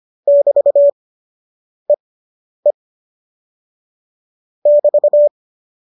Morse code, heard as a keyed tone, about 25 wpm, 570 Hz. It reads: <BT> E E <BT>